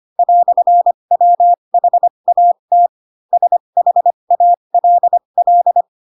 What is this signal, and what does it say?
Morse code, heard as a keyed tone, about 25 wpm, spaced as for 22 wpm, 700 Hz. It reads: "WHAT SHALL
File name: Pinocchio / Pinocchio ch3 18 25 ik1hge